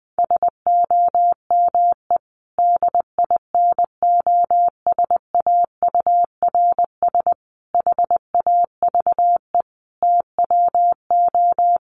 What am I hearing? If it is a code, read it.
Morse code, heard as a keyed tone, about 20 wpm, 700 Hz. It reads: SOME DINOSAURS HAVE TWO
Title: A Sound Of Thunder 20wpm